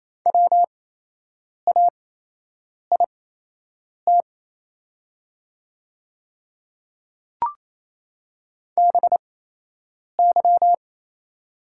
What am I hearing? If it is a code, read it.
Morse code, heard as a keyed tone, about 28 wpm, spaced as for 8 wpm, 700 Hz. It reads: WAIT BY
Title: 8x Character Spacing - Top 500 Words - 28wpm